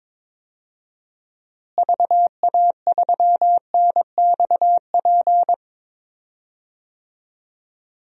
Morse code, heard as a keyed tone, about 22 wpm, 700 Hz. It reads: VA3NXP
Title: DXCC-100 Call Signs - 6 Characters - Repeated 2x - 22wpm